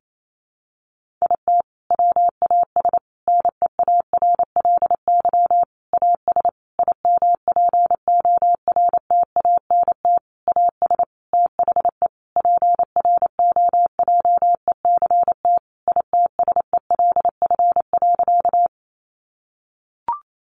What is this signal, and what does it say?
Morse code, heard as a keyed tone, about 28 wpm, 700 Hz. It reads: IT WAS NEARLY AS IMPORTANT AS THE PROJECT ITSELF.
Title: Sentences from Top 1400 Words 28wpm